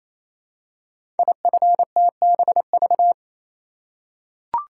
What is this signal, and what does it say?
Morse code, heard as a keyed tone, about 27 wpm, 700 Hz. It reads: IFTBV
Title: Code Groups - 5 Letters - 28wpm